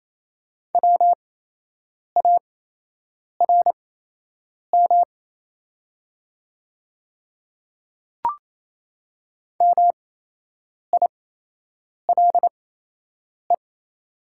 Morse code, heard as a keyed tone, about 28 wpm, spaced as for 8 wpm, 690 Hz. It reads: WARM MILE
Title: 8x Character Spacing - Top 500 Words - 28wpm